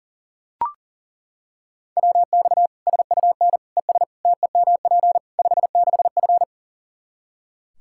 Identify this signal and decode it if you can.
Morse code, heard as a keyed tone, about 40 wpm, 700 Hz. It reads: WX SUN ES TEMP 56F
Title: Practice QSO - Exchange 2 - Info - 40wpm